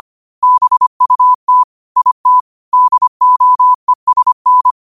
Morse code, heard as a keyed tone, about 25 wpm, 1000 Hz. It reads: BUT IT DOESN
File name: qotdfiles / 25wpm